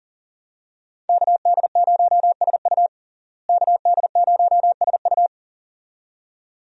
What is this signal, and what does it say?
Morse code, heard as a keyed tone, about 37 wpm, 700 Hz. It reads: KD0SU KD0SU
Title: USA Call Signs - 5 Characters - Repeated 2x - 40wpm